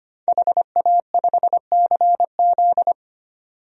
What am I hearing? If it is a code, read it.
Morse code, heard as a keyed tone, about 25 wpm, 700 Hz. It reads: HA5CZ
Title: DXCC-100 Call Signs - 5 Characters - 25wpm